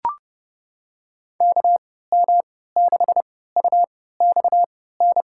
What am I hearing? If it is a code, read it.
Morse code, heard as a keyed tone, about 30 wpm, spaced as for 17 wpm, 700 Hz. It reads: KM6UXN
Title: USA Call Signs - 6 Characters - 3x Character Spacing - 30wpm